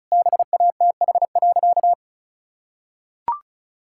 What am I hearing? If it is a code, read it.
Morse code, heard as a keyed tone, about 36 wpm, 700 Hz. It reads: BATH.
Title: Sentences from Top 1600 Words 35wpm